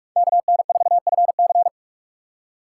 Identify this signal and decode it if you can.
Morse code, heard as a keyed tone, about 43 wpm, 700 Hz. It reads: KN4FC
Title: USA Call Signs - 5 Characters - 2x Character Spacing - 45wpm